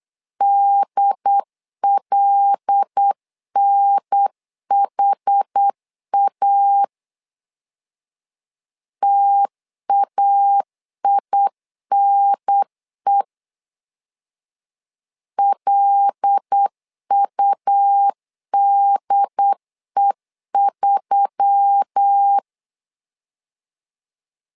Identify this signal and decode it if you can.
Morse code, heard as a keyed tone, about 8 wpm, 800 Hz. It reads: DLNHA TAINE LUDE3
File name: LEZIONI / 5 LBVC / 5 L 8wpm 2831